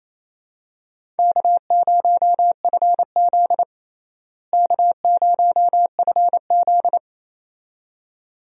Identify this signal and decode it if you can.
Morse code, heard as a keyed tone, about 28 wpm, 700 Hz. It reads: K0FZ K0FZ
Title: USA Call Signs - 4 Characters - Repeated 2x - 28wpm